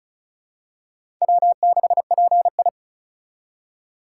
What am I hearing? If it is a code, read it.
Morse code, heard as a keyed tone, about 36 wpm, 700 Hz. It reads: W6PI
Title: USA Call Signs - 2x Character Spacing - 35wpm